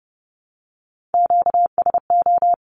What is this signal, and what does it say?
Morse code, heard as a keyed tone, about 30 wpm, 700 Hz. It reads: QSO
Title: Q-Codes Repeated 3x 30wpm